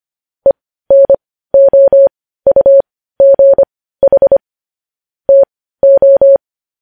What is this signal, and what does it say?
Morse code, heard as a keyed tone, about 25 wpm, spaced as for 15 wpm, 550 Hz. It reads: ENOUGH TO
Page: news-headlines-in-morse-code-at-15-wpm